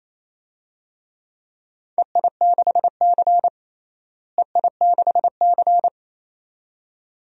Morse code, heard as a keyed tone, about 28 wpm, 700 Hz. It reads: EI6C EI6C
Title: DXCC-100 Call Signs - 4 Characters - Repeated 2x - 28wpm